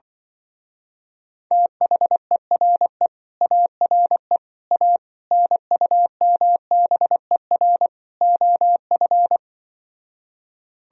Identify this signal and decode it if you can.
Morse code, heard as a keyed tone, about 25 wpm, 700 Hz. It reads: THERE ARE A NUMBER OF